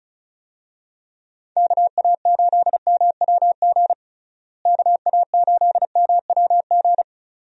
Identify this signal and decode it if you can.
Morse code, heard as a keyed tone, about 34 wpm, 700 Hz. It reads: KA8MWG KA8MWG
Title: USA Call Signs - 6 Characters - Repeated 2x - 35wpm